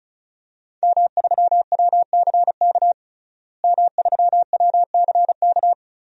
Morse code, heard as a keyed tone, about 36 wpm, 700 Hz. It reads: M3WCK M3WCK
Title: DXCC-100 Call Signs - 5 Characters - Repeated 2x - 35wpm